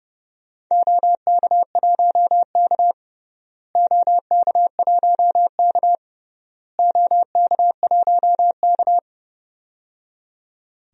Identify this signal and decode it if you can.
Morse code, heard as a keyed tone, about 31 wpm, 700 Hz. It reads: OK1K OK1K OK1K
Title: DXCC-100 Call Signs - 4 Characters - Repeated 3x - 30wpm